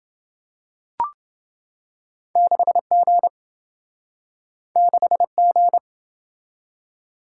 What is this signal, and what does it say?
Morse code, heard as a keyed tone, about 26 wpm, 700 Hz. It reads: 6G 6G